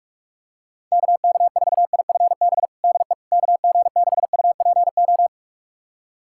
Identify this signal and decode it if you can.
Morse code, heard as a keyed tone, about 44 wpm, 700 Hz. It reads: KK4IFB DE KG6UPO
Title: USA Callsign DE Callsign - 6 Characters - 45wpm